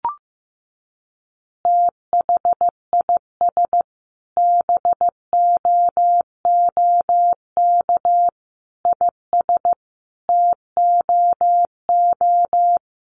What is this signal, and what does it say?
Morse code, heard as a keyed tone, about 15 wpm, 700 Hz. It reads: THIS BOOK IS TOO
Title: Additional Easy Phrases 15wpm